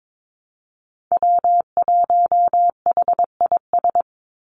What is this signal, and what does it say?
Morse code, heard as a keyed tone, about 22 wpm, 700 Hz. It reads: W1HIS
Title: Call Signs - 22wpm